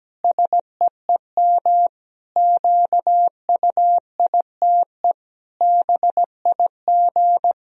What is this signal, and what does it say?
Morse code, heard as a keyed tone, about 17 wpm, 700 Hz. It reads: SEEM QUITE BIG
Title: Sentences from Top 1000 Words - Review Part 2 - 17wpm